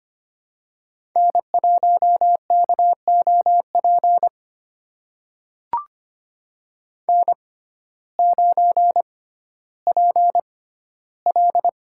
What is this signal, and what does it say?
Morse code, heard as a keyed tone, about 25 wpm, spaced as for 14 wpm, 700 Hz. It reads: N1KOP N9PL
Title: USA Call Signs - 6x Character Spacing - 25wpm